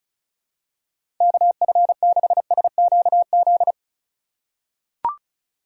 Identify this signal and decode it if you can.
Morse code, heard as a keyed tone, about 36 wpm, 700 Hz. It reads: KF6SQZ
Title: USA Call Signs - 3x Character Spacing - 35wpm